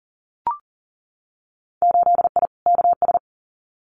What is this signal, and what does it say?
Morse code, heard as a keyed tone, about 40 wpm, 700 Hz. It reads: 8I KS
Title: Winter Field Day Exchanges Encore 40wpm